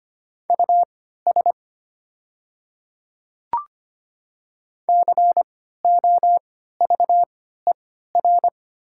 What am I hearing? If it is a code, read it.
Morse code, heard as a keyed tone, about 25 wpm, spaced as for 14 wpm, 700 Hz. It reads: US COVER
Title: 3x Character Spacing - Top 300 Words - 25wpm